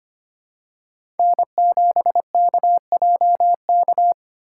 Morse code, heard as a keyed tone, about 25 wpm, 700 Hz. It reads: N7KJK
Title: USA Call Signs - 5 Characters - 2x Character Spacing - 25wpm